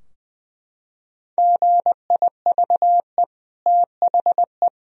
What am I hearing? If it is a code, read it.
Morse code, heard as a keyed tone, about 20 wpm, 700 Hz. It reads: GIVE THE
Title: Sentences from Top 1000 Words - Review Part 2 - 20wpm